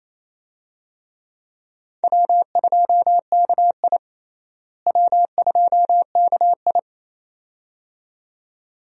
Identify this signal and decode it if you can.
Morse code, heard as a keyed tone, about 28 wpm, 700 Hz. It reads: W2KI W2KI
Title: USA Call Signs - 4 Characters - Repeated 2x - 28wpm